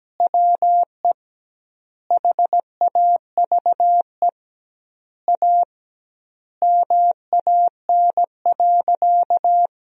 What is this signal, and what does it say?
Morse code, heard as a keyed tone, about 17 wpm, 700 Hz. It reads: WE HAVE A MAN.
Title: 2x Word-Spacing - Sentences from Top 300 Words - 17wpm